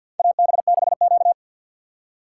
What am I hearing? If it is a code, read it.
Morse code, heard as a keyed tone, about 53 wpm, 700 Hz. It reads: AB6Q
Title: USA Call Signs - 4 Characters - 6x Character Spacing - 50wpm